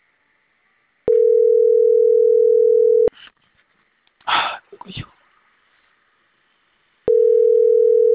Ringback tone (North America), 440 + 480 Hz.